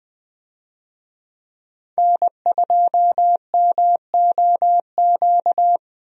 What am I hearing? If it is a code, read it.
Morse code, heard as a keyed tone, about 20 wpm, 700 Hz. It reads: N2MOQ
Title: USA Call Signs - 5 Characters - 20wpm